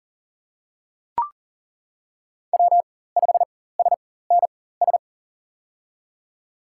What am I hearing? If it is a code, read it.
Morse code, heard as a keyed tone, about 40 wpm, spaced as for 19 wpm, 700 Hz. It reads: W5SNS
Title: USA Call Signs - 5 Characters - 4x Character Spacing - 40wpm